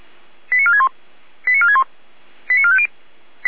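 Five-tone selective call (ZVEI), heard as quick runs of five tones, tones of about 70 ms. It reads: 89461 89461 89450
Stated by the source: Received by SDR, ZVEI five-tone call.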